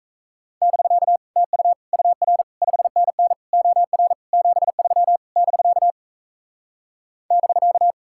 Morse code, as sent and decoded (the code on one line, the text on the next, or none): -...-.- / - ..- / ..- .-. / ..... -. -. / --- .-. / --... ...-- / -...-.- / -...-.-
<BK> TU UR 5NN OR 73 <BK> <BK>